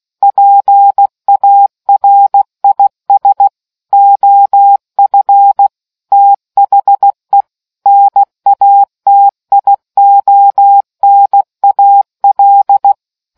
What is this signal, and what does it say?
Morse code, heard as a keyed tone, about 16 wpm, 800 Hz. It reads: PARIS OF THE NATIONAL